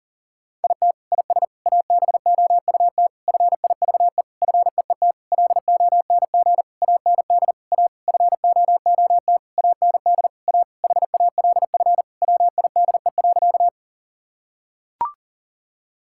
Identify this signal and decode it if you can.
Morse code, heard as a keyed tone, about 40 wpm, 700 Hz. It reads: IT IS ABOUT FIVE FEET LONG AND A FOOT AND A HALF WIDE.
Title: Sentences from Top 1000 Words - Review Part 6 - 40wpm